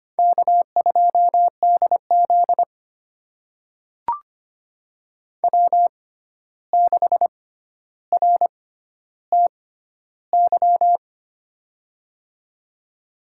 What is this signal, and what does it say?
Morse code, heard as a keyed tone, about 25 wpm, spaced as for 12 wpm, 700 Hz. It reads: K2DZ W6RTY
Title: USA Call Signs - 6x Character Spacing - 25wpm